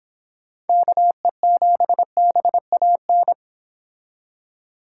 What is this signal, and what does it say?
Morse code, heard as a keyed tone, about 26 wpm, 700 Hz. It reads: KE7BAN